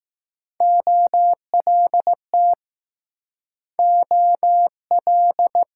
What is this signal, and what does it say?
Morse code, heard as a keyed tone, about 20 wpm, 700 Hz. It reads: OLT OL